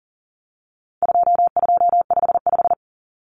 Morse code, heard as a keyed tone, about 40 wpm, 700 Hz. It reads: 2255
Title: Numbers - 4-Digit 40wpm